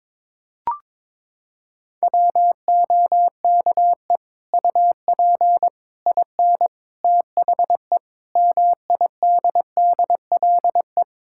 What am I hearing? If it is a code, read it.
Morse code, heard as a keyed tone, about 22 wpm, 700 Hz. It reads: WOKE UP IN THE MIDDLE
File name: practice / 22wpm / Sets of 8 Words - 22wpm